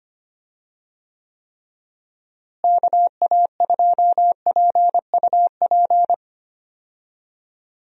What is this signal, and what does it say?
Morse code, heard as a keyed tone, about 25 wpm, 700 Hz. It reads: KA2PUP
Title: USA Call Signs - 6 Characters - Repeated 3x - 25wpm